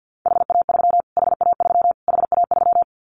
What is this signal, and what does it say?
Morse code, heard as a keyed tone, about 49 wpm, 700 Hz. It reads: HA3 HA3 HA3
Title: Callsign Prefixes for Top 25 DXCC - Repeated 3x - 50wpm